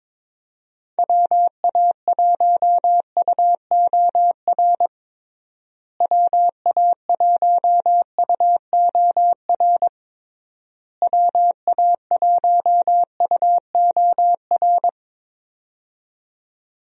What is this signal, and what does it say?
Morse code, heard as a keyed tone, about 22 wpm, 700 Hz. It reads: WA1UOR WA1UOR WA1UOR
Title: USA Call Signs - 6 Characters - Repeated 3x - 22wpm